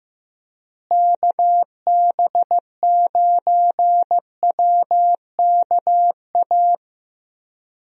Morse code, heard as a keyed tone, about 15 wpm, 700 Hz. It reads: KB9WKA
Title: USA Call Signs - 6 Characters - Repeated 3x - 15wpm